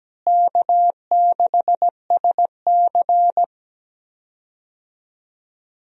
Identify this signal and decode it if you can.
Morse code, heard as a keyed tone, about 17 wpm, 700 Hz. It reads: K6SC
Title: USA Call Signs - 4 Characters - Repeated 2x - 17wpm